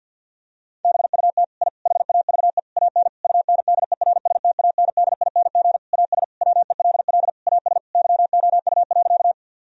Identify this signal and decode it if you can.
Morse code, heard as a keyed tone, about 50 wpm, 700 Hz. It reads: BUT I HAVE AN UNDERSTANDING AS WELL AS YOU.
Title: Sentences from Top 1000 Words - Review Part 3 - 50wpm